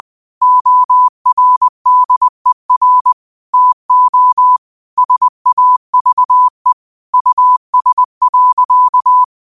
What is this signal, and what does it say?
Morse code, heard as a keyed tone, about 20 wpm, 1000 Hz. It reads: ORDER TO SAVE US.